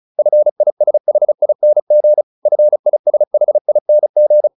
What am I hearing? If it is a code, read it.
Morse code, heard as a keyed tone, about 34 wpm, 600 Hz. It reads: FISHING FISHING